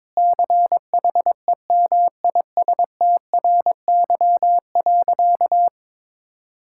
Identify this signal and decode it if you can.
Morse code, heard as a keyed tone, about 22 wpm, 700 Hz. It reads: CHEMISTRY.
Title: Sentences from Top 1000 Words - Review Part 6 - 22wpm